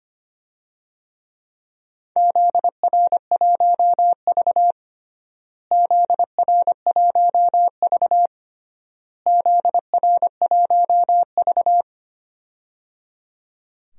Morse code, heard as a keyed tone, about 25 wpm, 700 Hz. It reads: ZR1V ZR1V ZR1V
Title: DXCC-100 Call Signs - 4 Characters - Repeated 3x - 25wpm